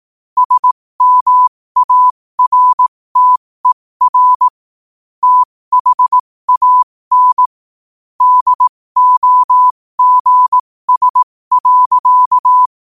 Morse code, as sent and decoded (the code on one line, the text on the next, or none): ... -- .- .-. - . .-. / - .... .- -. / -.. --- --. ... .-.-.-
SMARTER THAN DOGS.